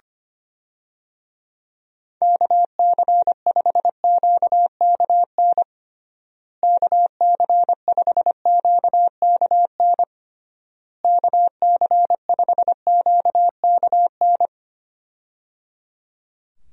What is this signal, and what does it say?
Morse code, heard as a keyed tone, about 25 wpm, 700 Hz. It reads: KC5QKN KC5QKN KC5QKN
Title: USA Call Signs - 6 Characters - Repeated 3x - 25wpm